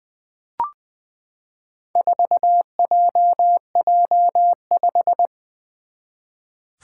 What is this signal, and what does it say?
Morse code, heard as a keyed tone, about 20 wpm, 700 Hz. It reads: 4JJ5
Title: Four Challenging Letter-Number Combinations 20wpm